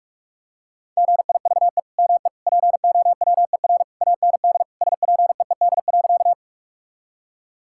Morse code, heard as a keyed tone, about 42 wpm, 700 Hz. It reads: GIVE ME POWER AND SPEED.
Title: Sentences from Top 1000 Words - Review Part 5 - 45wpm